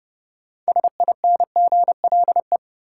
Morse code, heard as a keyed tone, about 30 wpm, 700 Hz. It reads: SINGLE
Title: Sentences from Top 700 Words - New Word Drill - 30wpm